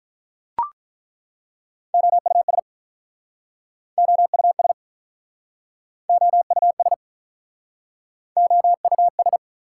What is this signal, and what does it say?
Morse code, heard as a keyed tone, about 47 wpm, 700 Hz. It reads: OUS OUS OUS OUS